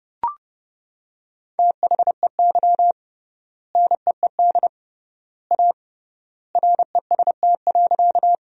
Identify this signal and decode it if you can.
Morse code, heard as a keyed tone, about 32 wpm, 700 Hz. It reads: THEY NEED A REST.
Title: 3x Word-Spacing - Sentences from Top 500 Words - 30wpm